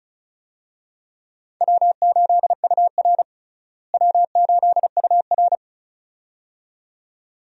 Morse code, heard as a keyed tone, about 36 wpm, 700 Hz. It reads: W8UR W8UR
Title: USA Call Signs - 4 Characters - Repeated 2x - 35wpm